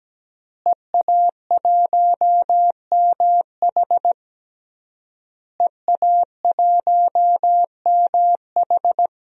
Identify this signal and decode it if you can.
Morse code, heard as a keyed tone, about 17 wpm, 700 Hz. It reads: EA1MH EA1MH
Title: DXCC-100 Call Signs - 5 Characters - Repeated 3x - 17wpm